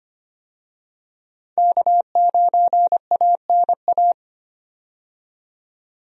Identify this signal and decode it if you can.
Morse code, heard as a keyed tone, about 25 wpm, 700 Hz. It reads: K9ANA
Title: USA Call Signs - 5 Characters - 25wpm